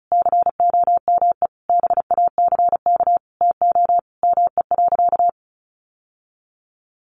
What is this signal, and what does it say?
Morse code, heard as a keyed tone, about 34 wpm, 700 Hz. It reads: COME BACK TO ME.
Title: Sentences from Top 600 Words 35wpm